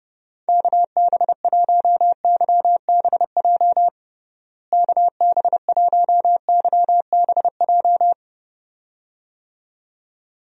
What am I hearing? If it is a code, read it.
Morse code, heard as a keyed tone, about 31 wpm, 700 Hz. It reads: KB1YBJ KB1YBJ
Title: USA Call Signs - 6 Characters - Repeated 2x - 30wpm